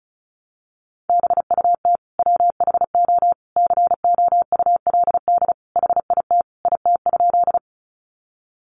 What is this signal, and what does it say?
Morse code, heard as a keyed tone, about 34 wpm, 700 Hz. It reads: BUT WHO COULD HIT IT?
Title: Sentences from Top 1200 Words 35wpm